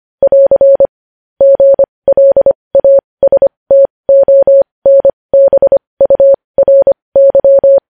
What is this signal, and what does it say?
Morse code, heard as a keyed tone, about 25 wpm, spaced as for 20 wpm, 550 Hz. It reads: <AR> GLASTONBURY
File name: podcast / News headlines-1x-25x20